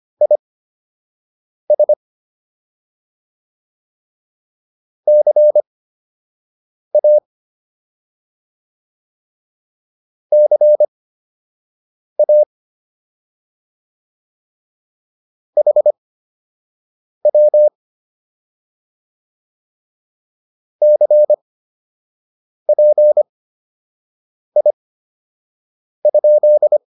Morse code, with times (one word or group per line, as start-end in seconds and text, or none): 0.2-1.9 IS
5.1-7.2 CA
10.3-12.4 CA
15.6-17.7 HW
20.8-26.8 CPI?